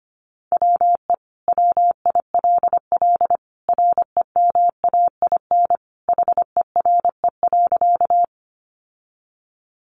Morse code, heard as a keyed tone, about 25 wpm, 700 Hz. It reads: WE WILL REMAIN HERE.
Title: Sentences from Top 1200 Words 25wpm